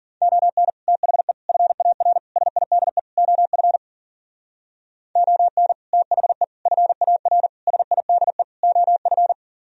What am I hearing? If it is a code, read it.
Morse code, heard as a keyed tone, about 49 wpm, 700 Hz. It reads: ON THE FAR SIDE OF ON THE FAR SIDE OF
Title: Speed-Racing - Sets of 5 Words 40wpm